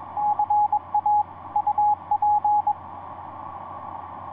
Morse code, heard as a keyed tone, about 22 wpm, 840 Hz, with radio noise. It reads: CA UP
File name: ce1-k7ca ea8tl etc 17m cw 10jan2010 1620